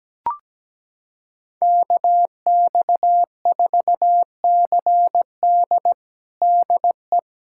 Morse code, heard as a keyed tone, about 17 wpm, 700 Hz. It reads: KX4CD DE
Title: USA Callsign DE Callsign - 5 Characters - 17wpm